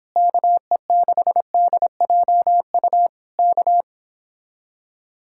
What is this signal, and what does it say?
Morse code, heard as a keyed tone, about 26 wpm, 700 Hz. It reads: KE6DJU K